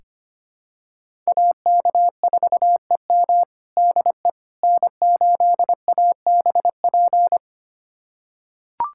Morse code, heard as a keyed tone, about 25 wpm, 700 Hz. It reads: AK4EM DE N8ABP
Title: USA Callsign DE Callsign - 5 Characters - 25wpm